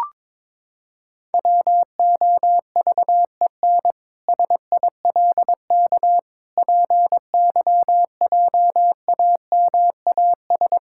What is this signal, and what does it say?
Morse code, heard as a keyed tone, about 22 wpm, 700 Hz. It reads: WOVEN SILK PYJAMAS
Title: Pangram Challenge v2 - 22wpm